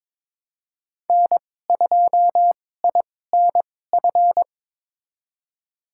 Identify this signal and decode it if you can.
Morse code, heard as a keyed tone, about 22 wpm, spaced as for 16 wpm, 700 Hz. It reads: N2INF
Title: USA Call Signs - 2x Character Spacing - 22wpm